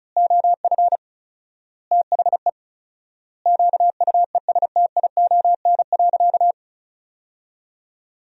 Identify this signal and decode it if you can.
Morse code, heard as a keyed tone, about 34 wpm, 700 Hz. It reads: OF THE QUESTION.
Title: 4x Word-Spacing - Sentences from Top 400 Words - 35wpm